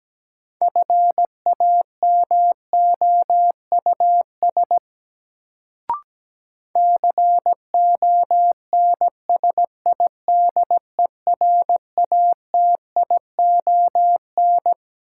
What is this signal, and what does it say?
Morse code, heard as a keyed tone, about 17 wpm, 700 Hz. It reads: FAMOUS CONSIDERATION
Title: Sentences from Top 1500 Words - New Word Drill Alt - 17wpm